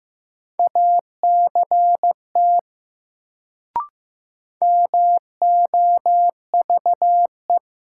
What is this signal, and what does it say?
Morse code, heard as a keyed tone, about 15 wpm, 700 Hz. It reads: ACT MOVE
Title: Sentences from Top 200 Words - New Word Drill - 15wpm